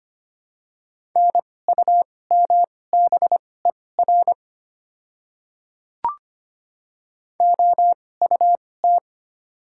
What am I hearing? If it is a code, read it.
Morse code, heard as a keyed tone, about 25 wpm, spaced as for 18 wpm, 690 Hz. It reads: NUMBER OUT
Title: 2x Character Spacing - Top 100 Words - 25wpm